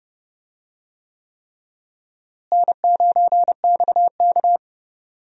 Morse code, heard as a keyed tone, about 31 wpm, 700 Hz. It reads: N9XK
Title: USA Call Signs - 4 Characters - Repeated 3x - 30wpm